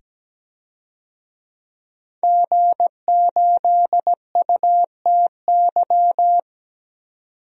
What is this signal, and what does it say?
Morse code, heard as a keyed tone, about 17 wpm, 700 Hz. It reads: G8UTY
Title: DXCC-100 Call Signs - 5 Characters - 17wpm